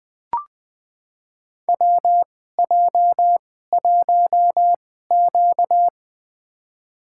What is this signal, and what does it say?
Morse code, heard as a keyed tone, about 20 wpm, spaced as for 15 wpm, 700 Hz. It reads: WJ1Q
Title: USA Call Signs - 4 Characters - 2x Character Spacing - 20wpm